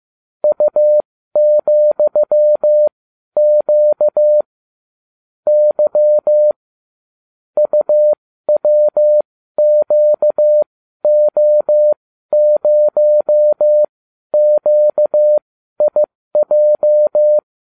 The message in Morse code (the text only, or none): U,Q Y UWQO0QIJ